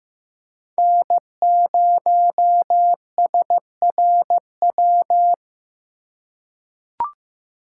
Morse code, heard as a keyed tone, about 15 wpm, 700 Hz. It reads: N0SRW
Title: USA Call Signs - 5 Characters - 3x Character Spacing - 15wpm